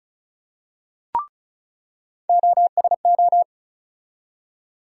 Morse code, heard as a keyed tone, about 34 wpm, 700 Hz. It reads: OSO
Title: USA Call Sign Postfixes - 3 Letters 35wpm